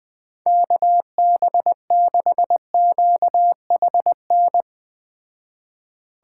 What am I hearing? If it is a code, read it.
Morse code, heard as a keyed tone, about 19 wpm, 700 Hz. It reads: KB6QHN
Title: Speed Racing - USA 2x3 Call Signs 17wpm